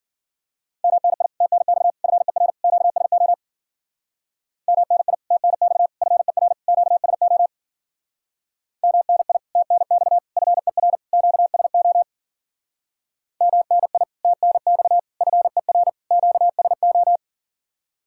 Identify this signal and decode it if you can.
Morse code, heard as a keyed tone, about 52 wpm, 700 Hz. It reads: MNI TNX FER QSO MNI TNX FER QSO MNI TNX FER QSO MNI TNX FER QSO